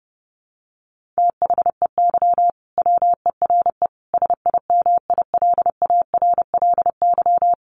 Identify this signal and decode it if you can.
Morse code, heard as a keyed tone, about 30 wpm, 700 Hz. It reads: THEY WERE SIMILARLY
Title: Sentences from Top 1400 Words 30wpm